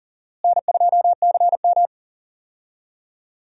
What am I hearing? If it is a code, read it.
Morse code, heard as a keyed tone, about 40 wpm, 700 Hz. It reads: N2CM
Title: USA Call Signs - 4 Characters - Repeated 3x - 40wpm